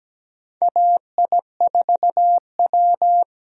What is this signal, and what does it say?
Morse code, heard as a keyed tone, about 17 wpm, 700 Hz. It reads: AI4W
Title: USA Call Signs - 4 Characters - Repeated 2x - 17wpm